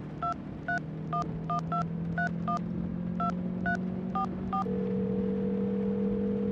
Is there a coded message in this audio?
Yes: DTMF from 0.2 to 4.6 s, heard as telephone touch tones, tones of about 110 ms. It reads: 23112312344